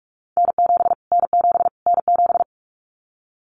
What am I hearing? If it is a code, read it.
Morse code, heard as a keyed tone, about 44 wpm, 700 Hz. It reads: N7 N7 N7
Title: Callsign Prefixes for Top 25 DXCC - Repeated 3x - 45wpm